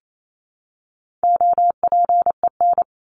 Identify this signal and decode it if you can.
Morse code, heard as a keyed tone, about 28 wpm, 700 Hz. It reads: OPEN
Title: Top 300 Words 28wpm